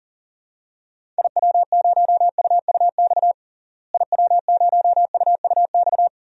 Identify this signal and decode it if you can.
Morse code, heard as a keyed tone, about 40 wpm, 700 Hz. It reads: IW0UUX IW0UUX
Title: DXCC-100 Call Signs - 6 Characters - Repeated 2x - 40wpm